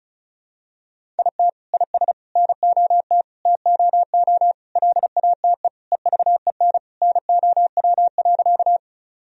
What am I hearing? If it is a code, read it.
Morse code, heard as a keyed tone, about 36 wpm, 700 Hz. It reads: IT IS NOT TOO LATE EVEN NOW.
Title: Sentences from Top 1000 Words - Review Part 1 - 35wpm